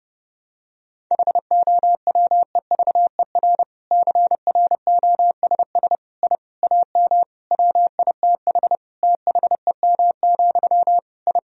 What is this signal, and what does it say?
Morse code, heard as a keyed tone, about 30 wpm, 700 Hz. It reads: HOWEVER CROSS I AM WITH THEM, I
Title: Sentences from Top 1000 Words - Review Part 1 - 30wpm